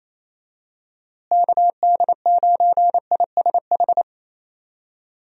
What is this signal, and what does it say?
Morse code, heard as a keyed tone, about 28 wpm, 700 Hz. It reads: KD9ISH
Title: USA Call Signs - 6 Characters - Repeated 3x - 28wpm